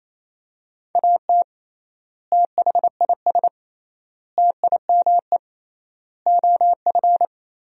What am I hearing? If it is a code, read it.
Morse code, heard as a keyed tone, about 28 wpm, 700 Hz. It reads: AT THIS TIME OF
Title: 3x Word-Spacing - Sentences from Top 300 Words - 28wpm